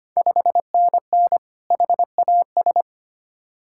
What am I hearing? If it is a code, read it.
Morse code, heard as a keyed tone, about 25 wpm, 700 Hz. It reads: 5NN HAS
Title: Ontario QSO Party Exchange 25wpm